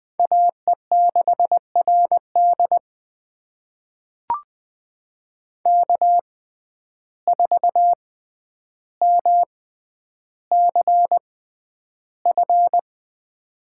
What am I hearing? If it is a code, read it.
Morse code, heard as a keyed tone, about 20 wpm, spaced as for 10 wpm, 700 Hz. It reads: AE6RD K4MCF
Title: USA Call Signs - 5 Characters - 6x Character Spacing - 20wpm